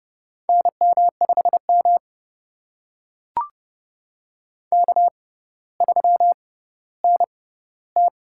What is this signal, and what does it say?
Morse code, heard as a keyed tone, about 32 wpm, spaced as for 16 wpm, 700 Hz. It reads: NM5M K3NT
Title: USA Call Signs - 6x Character Spacing - 30wpm